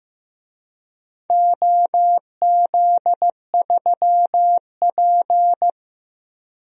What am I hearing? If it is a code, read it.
Morse code, heard as a keyed tone, about 15 wpm, 700 Hz. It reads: OZ3P
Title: DXCC-100 Call Signs - 4 Characters - Repeated 3x - 15wpm